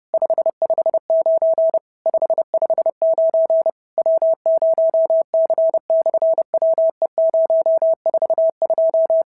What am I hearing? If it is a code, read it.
Morse code, heard as a keyed tone, about 30 wpm, 650 Hz. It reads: 559 559 W0C/WE042
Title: SOTA QSO Practice 30wpm